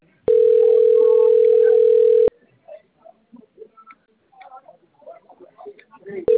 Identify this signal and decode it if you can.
Ringback tone (North America), 440 + 480 Hz, with speech.
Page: Limeira V